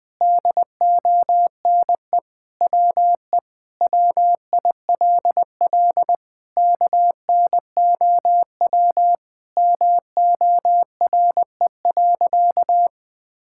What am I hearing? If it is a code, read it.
Morse code, heard as a keyed tone, about 20 wpm, 700 Hz. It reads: DONE WE WILL KNOW MORE.
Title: Sentences with 6-Word Phrases - 20wpm